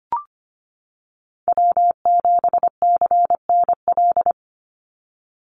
Morse code, heard as a keyed tone, about 25 wpm, 700 Hz. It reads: W7CNL
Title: NA SOTA Chaser Callsigns 25wpm